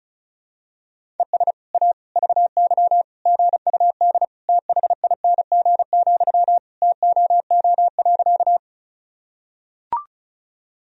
Morse code, heard as a keyed tone, about 34 wpm, 700 Hz. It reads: ES A VY GUD THING, TOO.
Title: Sentences from Top 200 Words and Abbreviations 35wpm